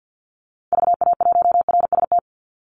Morse code, heard as a keyed tone, about 50 wpm, 700 Hz. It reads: VA1RST
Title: Call Signs Repeated 3x - 50wpm